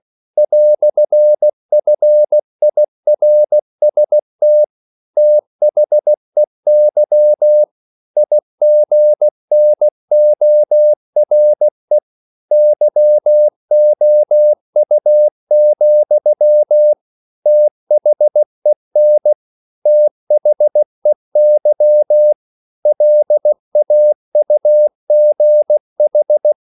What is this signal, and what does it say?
Morse code, heard as a keyed tone, about 16 wpm, 600 Hz. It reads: "FIRST THEY IGNORE YOU, THEN THEY LAUGH
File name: CW / Practice / EnglishQuotations0216